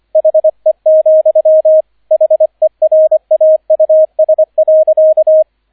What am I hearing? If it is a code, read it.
Morse code, heard as a keyed tone, about 25 wpm, 630 Hz, with radio noise. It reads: HE, HERAUS.
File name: 003 Dritter Streich 25wpm